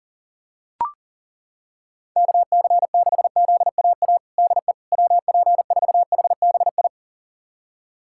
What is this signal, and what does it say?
Morse code, heard as a keyed tone, about 37 wpm, 700 Hz. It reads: KC6ZAA DE WP4HBI
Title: USA Callsign DE Callsign - 6 Characters - 40wpm